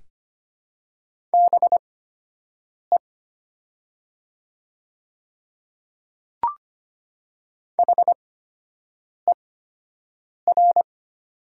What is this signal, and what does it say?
Morse code, heard as a keyed tone, about 25 wpm, spaced as for 7 wpm, 700 Hz. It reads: BE HER
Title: 8x Character Spacing - Top 100 Words - 25wpm